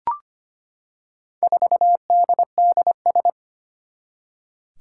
Morse code, heard as a keyed tone, about 25 wpm, 700 Hz. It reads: 4DDS
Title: Four Challenging Letter-Number Combinations 25wpm